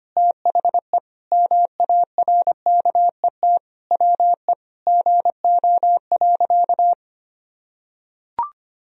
Morse code, heard as a keyed tone, about 25 wpm, 700 Hz. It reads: THE MARKET WE GO.
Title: Sentences from Top 1000 Words - Review Part 3 - 25wpm